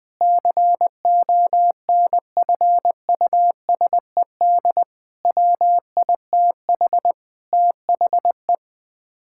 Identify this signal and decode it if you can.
Morse code, heard as a keyed tone, about 20 wpm, 700 Hz. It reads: CONFUSED WITH THE
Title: Sets of 6 Words - 20wpm